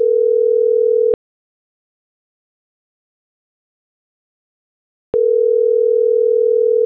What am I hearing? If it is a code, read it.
Ringback tone (North America), 440 + 480 Hz.